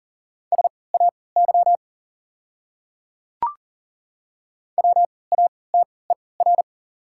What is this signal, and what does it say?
Morse code, heard as a keyed tone, about 40 wpm, spaced as for 23 wpm, 700 Hz. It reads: SAY WATER
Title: 3x Character Spacing - Top 200 Words - 40wpm